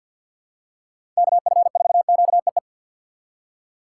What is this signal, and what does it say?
Morse code, heard as a keyed tone, about 47 wpm, 700 Hz. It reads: KF4QEE
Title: USA Call Signs - 6x Character Spacing - 50wpm